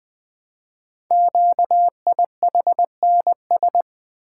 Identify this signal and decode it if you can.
Morse code, heard as a keyed tone, about 20 wpm, 700 Hz. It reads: QIHNS
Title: Code Groups - 5 Letters - 20wpm